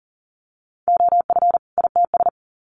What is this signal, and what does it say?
Morse code, heard as a keyed tone, about 40 wpm, 700 Hz. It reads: OF ITS
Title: Sets of 2 Words 40wpm